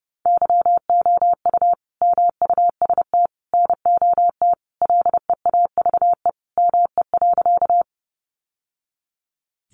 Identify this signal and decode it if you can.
Morse code, heard as a keyed tone, about 30 wpm, 700 Hz. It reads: YOU MUST NOT LEAVE ME.
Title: Sentences from Top 600 Words 30wpm